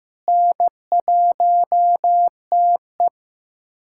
Morse code, heard as a keyed tone, about 15 wpm, 700 Hz. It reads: N1TE
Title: USA Callsign DE Callsign - 4 Characters - 15wpm